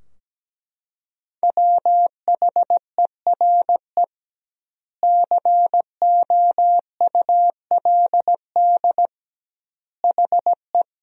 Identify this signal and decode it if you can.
Morse code, heard as a keyed tone, about 17 wpm, 700 Hz. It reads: WHERE COULD HE
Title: 2x Word-Spacing - Sentences from Top 200 Words - 17wpm